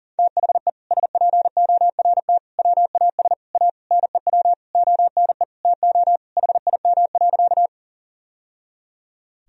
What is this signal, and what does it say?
Morse code, heard as a keyed tone, about 40 wpm, 700 Hz. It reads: THE SPORT WAS A NEW ONE TO HIM.
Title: Sentences from Top 1000 Words - Review Part 4 - 40wpm